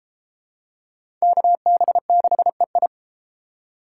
Morse code, heard as a keyed tone, about 30 wpm, 700 Hz. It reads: KB6EI